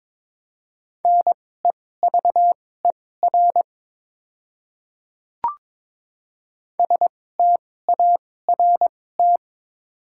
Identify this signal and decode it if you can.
Morse code, heard as a keyed tone, about 22 wpm, spaced as for 16 wpm, 700 Hz. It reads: NEVER START
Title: 2x Character Spacing - Top 300 Words - 22wpm